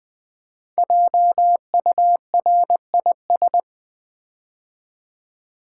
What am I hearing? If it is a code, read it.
Morse code, heard as a keyed tone, about 20 wpm, 700 Hz. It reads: JURIS
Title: Build A Word - 5 Letters At A Time - 12-Letter Words - 20wpm